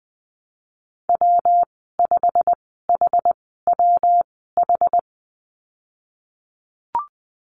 Morse code, heard as a keyed tone, about 20 wpm, spaced as for 15 wpm, 700 Hz. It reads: W5HWH
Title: 147 Lesson 24g - Letter y - Call Signs 20wpm